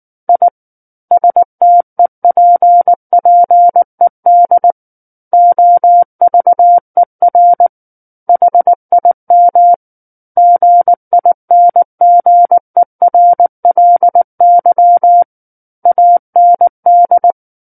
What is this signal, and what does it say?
Morse code, heard as a keyed tone, about 19 wpm, 700 Hz. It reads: I STEPPED OVER HIM GINGERLY AND
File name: Book One / 10-Chapter 10 - 19 WPM